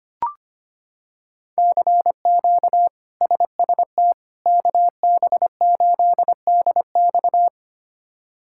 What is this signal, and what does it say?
Morse code, heard as a keyed tone, about 25 wpm, 700 Hz. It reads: CQ SST KB8DX
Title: K1USN SST QSO Practice - 25wpm